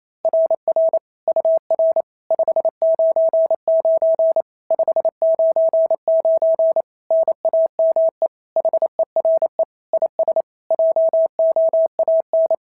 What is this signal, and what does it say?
Morse code, heard as a keyed tone, about 28 wpm, 690 Hz. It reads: RR UR 599 599 NAME HERE IS JOAN
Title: SOTA QSO Practice 28wpm